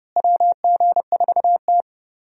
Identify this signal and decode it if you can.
Morse code, heard as a keyed tone, about 30 wpm, 700 Hz. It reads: WG4T
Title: USA Call Signs - 4 Characters - Repeated 2x - 30wpm